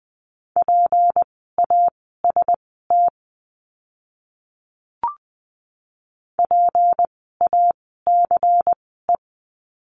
Morse code, heard as a keyed tone, about 20 wpm, spaced as for 14 wpm, 700 Hz. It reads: PAST PACE
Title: 153 Lesson 25f - Letter p - Words 20wpm